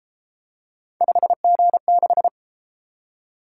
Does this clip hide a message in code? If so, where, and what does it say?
Yes: Morse code from 1.0 to 2.3 s, heard as a keyed tone, about 30 wpm, 700 Hz. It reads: 5G6